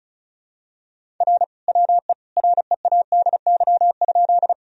Morse code, heard as a keyed tone, about 36 wpm, 700 Hz. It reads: R WE READY?
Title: Sentences from Top 400 Words and Abbreviations 35wpm